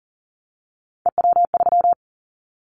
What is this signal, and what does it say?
Morse code, heard as a keyed tone, about 38 wpm, 700 Hz. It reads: EW3
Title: Callsign Prefixes for Top 50 DXCC - 40wpm